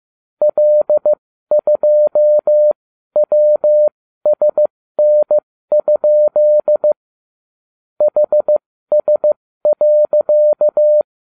Morse code, heard as a keyed tone, about 15 wpm, spaced as for 13 wpm, 600 Hz. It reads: L2WSN? HS.
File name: cw / ubacw34 3